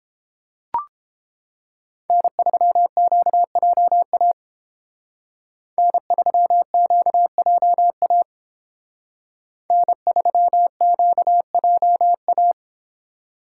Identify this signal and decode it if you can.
Morse code, heard as a keyed tone, about 30 wpm, 700 Hz. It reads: N3QJA N3QJA N3QJA